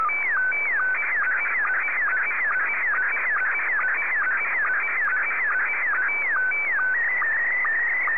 SSTV (slow-scan television), Scottie 1.